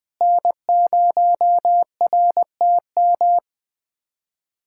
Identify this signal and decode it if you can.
Morse code, heard as a keyed tone, about 20 wpm, 700 Hz. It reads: N0RTM
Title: USA Callsign DE Callsign - 5 Characters - 20wpm